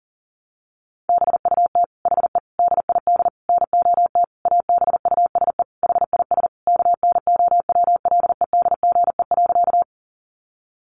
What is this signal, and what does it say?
Morse code, heard as a keyed tone, about 37 wpm, 700 Hz. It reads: BUT HE DID NOT ABUSE HIS KNOWLEDGE.
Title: Sentences from Top 1300 Words 40wpm